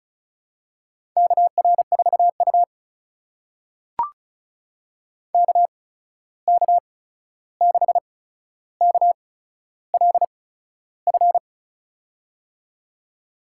Morse code, heard as a keyed tone, about 36 wpm, spaced as for 13 wpm, 700 Hz. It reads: KR4U KK6KLF
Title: USA Call Signs - 8x Character Spacing - 35wpm